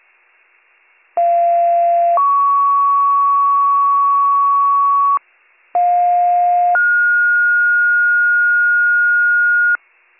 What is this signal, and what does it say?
Two-tone page, heard as a one-second tone, then a three-second tone, about 690 then 1120 Hz, about 710 then 1430 Hz. It is heard from 1.2 to 9.8 s.